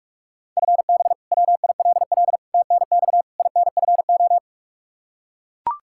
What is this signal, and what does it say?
Morse code, heard as a keyed tone, about 41 wpm, 700 Hz. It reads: FB WILL TNX INFO
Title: Practice QSO - Exchange 3 - More Info - 45wpm